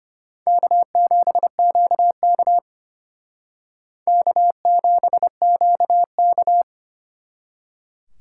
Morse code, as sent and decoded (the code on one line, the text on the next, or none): -.- --... --.- -.- / -.- --... --.- -.-
K7QK K7QK